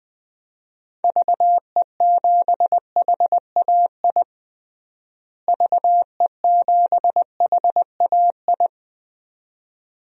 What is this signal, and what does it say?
Morse code, heard as a keyed tone, about 21 wpm, 700 Hz. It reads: VE7HAI VE7HAI
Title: DXCC-100 Call Signs - 6 Characters - Repeated 2x - 20wpm